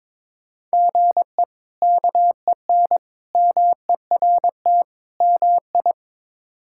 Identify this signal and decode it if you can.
Morse code, heard as a keyed tone, about 22 wpm, 700 Hz. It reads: GE KEN MERT MI
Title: K1USN SST QSO Practice - 22wpm